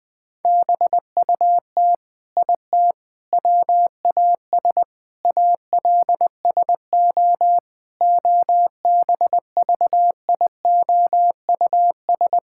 Morse code, heard as a keyed tone, about 20 wpm, 700 Hz. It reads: BUT IT WAS ALSO OBVIOUS
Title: Sentences from Top 1600 Words 20wpm